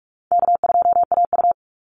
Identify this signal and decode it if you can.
Morse code, heard as a keyed tone, about 46 wpm, 700 Hz. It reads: K2AU
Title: Easy Call Signs - 45wpm